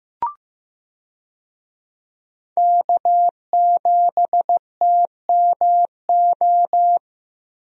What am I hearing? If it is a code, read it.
Morse code, heard as a keyed tone, about 15 wpm, 700 Hz. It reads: K7TMO
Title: USA Call Signs - 5 Characters - Repeated 2x - 15wpm